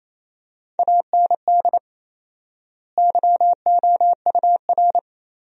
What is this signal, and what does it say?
Morse code, heard as a keyed tone, about 28 wpm, 700 Hz. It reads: AND YOUR
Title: 4x Word-Spacing - Sentences from Top 300 Words - 28wpm